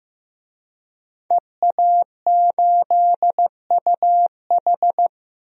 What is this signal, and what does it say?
Morse code, heard as a keyed tone, about 15 wpm, 700 Hz. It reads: EA8UH
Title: DXCC-100 Call Signs - 5 Characters - Repeated 2x - 15wpm